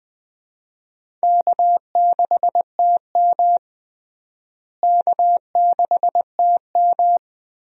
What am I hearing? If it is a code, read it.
Morse code, heard as a keyed tone, about 20 wpm, 700 Hz. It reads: K6TM K6TM
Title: USA Call Signs - 4 Characters - Repeated 2x - 20wpm